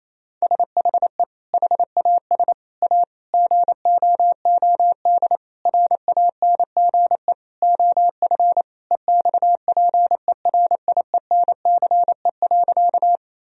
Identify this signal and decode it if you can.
Morse code, heard as a keyed tone, about 28 wpm, 700 Hz. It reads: SHE HAS A GOOD RANGE OF EXPERIENCE.
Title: Sentences from Top 1000 Words - Review Part 4 - 28wpm